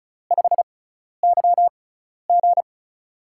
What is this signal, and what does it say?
Morse code, heard as a keyed tone, about 36 wpm, spaced as for 12 wpm, 700 Hz. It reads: 5YG
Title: USA Call Signs - 4 Characters - 6x Character Spacing - 35wpm